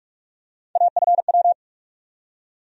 Morse code, heard as a keyed tone, about 46 wpm, 700 Hz. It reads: AFW
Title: USA Call Sign Postfixes - 3 Letters 45wpm